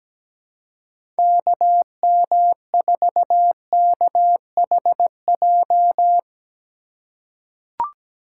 Morse code, heard as a keyed tone, about 17 wpm, 700 Hz. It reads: KM4KHJ
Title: USA Call Signs - 6 Characters - 4x Character Spacing - 17wpm